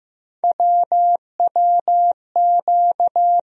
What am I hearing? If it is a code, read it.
Morse code, heard as a keyed tone, about 15 wpm, 700 Hz. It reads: WWQ
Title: USA Call Sign Postfixes - 3 Letters 15wpm